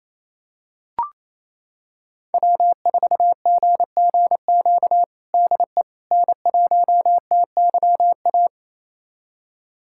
Morse code, heard as a keyed tone, about 27 wpm, 700 Hz. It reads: W4GGQ DE N1TYA
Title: USA Callsign DE Callsign - 5 Characters - 28wpm